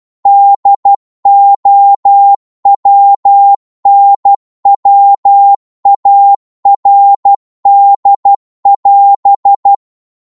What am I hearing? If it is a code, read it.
Morse code, heard as a keyed tone, about 12 wpm, 800 Hz. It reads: DOWNWARD<AS>
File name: Morse / psg-12wpm